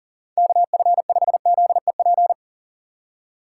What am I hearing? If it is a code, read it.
Morse code, heard as a keyed tone, about 40 wpm, 700 Hz. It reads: KF5ZEP
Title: USA Call Signs - 6 Characters - 6x Character Spacing - 40wpm